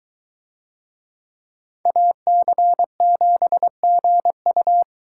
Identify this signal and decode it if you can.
Morse code, heard as a keyed tone, about 22 wpm, 700 Hz. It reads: AC7GU